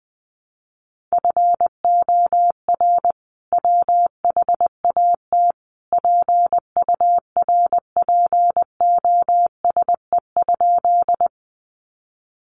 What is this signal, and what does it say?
Morse code, heard as a keyed tone, about 20 wpm, 700 Hz. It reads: FOR WHAT PURPOSE?
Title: Sentences from Top 800 Words 20wpm